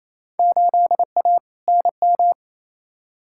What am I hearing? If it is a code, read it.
Morse code, heard as a keyed tone, about 27 wpm, 700 Hz. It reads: 8A NM
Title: Field Day Exchanges Encore - 28wpm